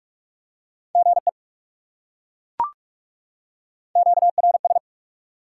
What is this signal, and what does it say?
Morse code, heard as a keyed tone, about 38 wpm, 700 Hz. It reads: GE QRS